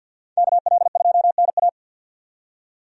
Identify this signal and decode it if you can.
Morse code, heard as a keyed tone, about 51 wpm, 700 Hz. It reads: KL2NA
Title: USA Call Signs - 5 Characters - 50wpm